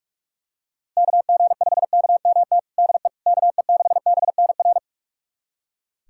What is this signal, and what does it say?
Morse code, heard as a keyed tone, about 46 wpm, 700 Hz. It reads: KG5KMT DE KE6BNR
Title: USA Callsign DE Callsign - 6 Characters - 45wpm